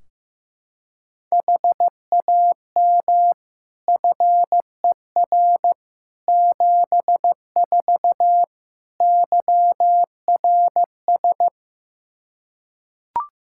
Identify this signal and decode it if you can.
Morse code, heard as a keyed tone, about 15 wpm, 700 Hz. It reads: HAM FER 74 YRS
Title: Practice QSO - Exchange 3 - More Info - 15wpm